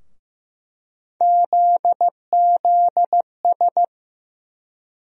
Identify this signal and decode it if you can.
Morse code, heard as a keyed tone, about 15 wpm, 700 Hz. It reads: ZZS
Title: USA Call Sign Postfixes - 3 Letters 15wpm